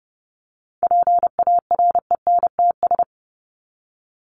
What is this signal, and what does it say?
Morse code, heard as a keyed tone, about 30 wpm, 700 Hz. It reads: PARENTS
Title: ES:S Suffix - Top 100 Words with Base Words 30wpm